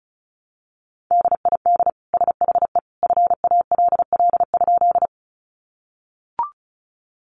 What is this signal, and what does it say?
Morse code, heard as a keyed tone, about 34 wpm, 700 Hz. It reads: DID SHE FALL?
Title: Sentences from Top 500 Words 35wpm